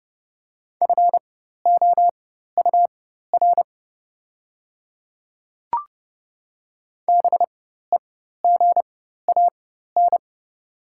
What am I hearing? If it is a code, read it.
Morse code, heard as a keyed tone, about 29 wpm, spaced as for 14 wpm, 690 Hz. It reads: FOUR BEGAN
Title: 4x Character Spacing - Top 500 Words - 30wpm